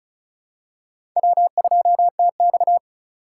Morse code, heard as a keyed tone, about 36 wpm, 700 Hz. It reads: W2TX
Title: USA Call Signs - 4 Characters - 2x Character Spacing - 35wpm